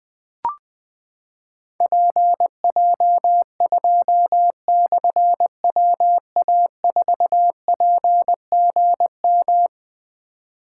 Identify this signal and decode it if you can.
Morse code, heard as a keyed tone, about 20 wpm, 700 Hz. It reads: PJ2/WA4PGM
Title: Difficult Call Signs 20wpm